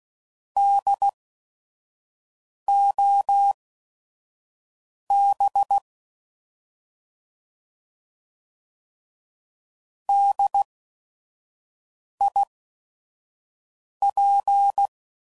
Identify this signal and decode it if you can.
Morse code, heard as a keyed tone, about 16 wpm, spaced as for 5 wpm, 800 Hz. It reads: DOB DIP
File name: Prefixes / By First Letter / Ds